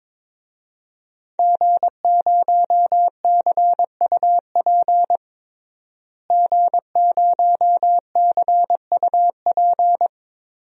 Morse code, heard as a keyed tone, about 22 wpm, 700 Hz. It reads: G0CUP G0CUP
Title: DXCC-100 Call Signs - 5 Characters - Repeated 3x - 22wpm